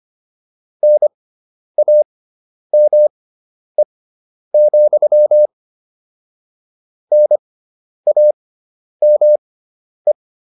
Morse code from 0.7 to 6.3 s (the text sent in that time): NAME,